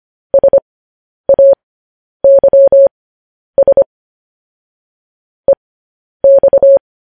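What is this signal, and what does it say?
Morse code, heard as a keyed tone, about 25 wpm, spaced as for 10 wpm, 550 Hz. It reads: SAYS EX
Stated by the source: News headlines sent in Morse at 10 wpm.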